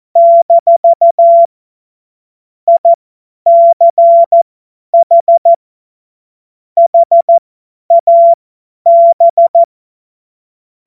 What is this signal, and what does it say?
Morse code, heard as a keyed tone, about 14 wpm, spaced as for 10 wpm, 680 Hz. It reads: - ICH HAB
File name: Witz-29-Juli-2025-50-F70